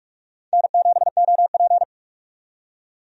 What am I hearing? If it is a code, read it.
Morse code, heard as a keyed tone, about 46 wpm, 700 Hz. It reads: N7OP
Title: USA Call Signs - 4 Characters - 45wpm